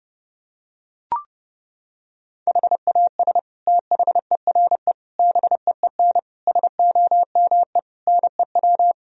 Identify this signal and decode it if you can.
Morse code, heard as a keyed tone, about 30 wpm, 700 Hz. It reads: HAS THERE BEEN SOME NEW
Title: Sentences from Top 1000 Words - Review Part 4 - 30wpm